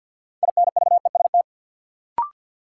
Morse code, heard as a keyed tone, about 46 wpm, 700 Hz. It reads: INVEST
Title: Sentences from Top 1100 Words - New Word Drill Alt - 50wpm